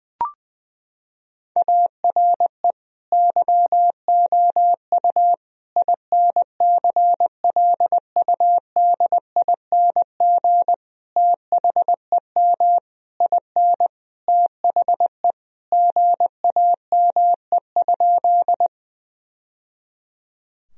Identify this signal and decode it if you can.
Morse code, heard as a keyed tone, about 20 wpm, 700 Hz. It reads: ARE YOU INCLUDING THEM IN THE GAME?
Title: Sentences from Top 1000 Words - Review Part 4 - 20wpm